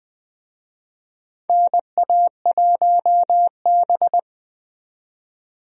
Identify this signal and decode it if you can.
Morse code, heard as a keyed tone, about 20 wpm, 700 Hz. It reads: NA1B
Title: USA Call Signs - 4 Characters - 20wpm